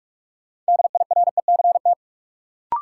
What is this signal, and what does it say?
Morse code, heard as a keyed tone, about 47 wpm, 700 Hz. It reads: DIRECT
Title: Sentences from Top 400 Words - New Word Drill - 45wpm